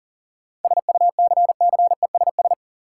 Morse code, heard as a keyed tone, about 40 wpm, 700 Hz. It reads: SUCCESS
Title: Sentences from Top 900 Words - New Word Drill - 40wpm